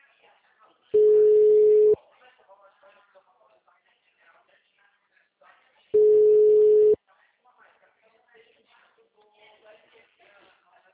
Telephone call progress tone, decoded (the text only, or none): ringback tone